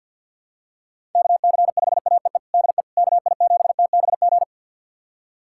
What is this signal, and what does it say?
Morse code, heard as a keyed tone, about 50 wpm, 700 Hz. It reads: KC5AEE DE KI7TBG
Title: USA Callsign DE Callsign - 6 Characters - 50wpm